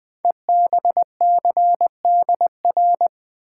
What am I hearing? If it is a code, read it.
Morse code, heard as a keyed tone, about 20 wpm, 700 Hz. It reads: EBCDR
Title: Code Groups - 5 Letters - 20wpm